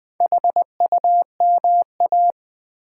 Morse code, heard as a keyed tone, about 20 wpm, 700 Hz. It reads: HUMA
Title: Build A Word - 1 Letter At A Time - 3 to 5-Letter Words - 20wpm